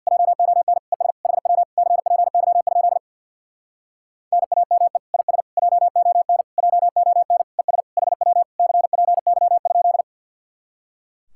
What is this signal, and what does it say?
Morse code, heard as a keyed tone, about 67 wpm, 700 Hz. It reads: JON ES HW CPY? NAME IS JON JON ES HW CPY?